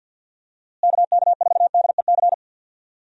Morse code, heard as a keyed tone, about 48 wpm, 700 Hz. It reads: KK4DEC
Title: USA Call Signs - 6 Characters - Repeated 2x - 50wpm